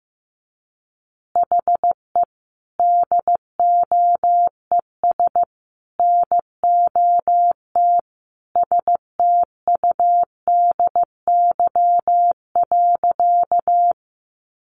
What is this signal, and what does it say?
Morse code, heard as a keyed tone, about 14 wpm, 700 Hz. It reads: HE DOES NOT STUDY.
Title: Additional Easy Phrases 15wpm